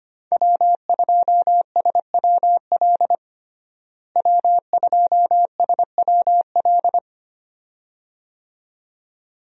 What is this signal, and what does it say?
Morse code, heard as a keyed tone, about 25 wpm, 700 Hz. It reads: W2SWL W2SWL
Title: USA Call Signs - 5 Characters - Repeated 2x - 25wpm